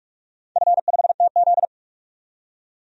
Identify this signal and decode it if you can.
Morse code, heard as a keyed tone, about 46 wpm, 700 Hz. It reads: F5TZ
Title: DXCC-100 Call Signs - 4 Characters - 45wpm